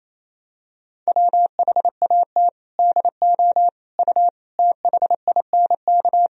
Morse code, heard as a keyed tone, about 28 wpm, 700 Hz. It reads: WHAT DO U THINK
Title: Sentences from Top 200 Words and Abbreviations 28wpm